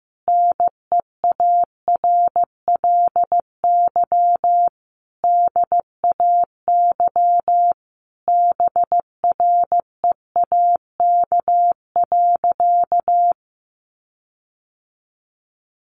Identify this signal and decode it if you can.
Morse code, heard as a keyed tone, about 15 wpm, 700 Hz. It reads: NEARLY DAY BREAK.
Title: Sentences from Top 1300 Words 15wpm